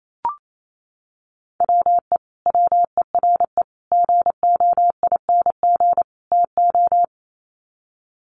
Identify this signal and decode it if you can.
Morse code, heard as a keyed tone, about 28 wpm, 700 Hz. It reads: WE WERE GOING TO
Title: Sets of 4 Words 28wpm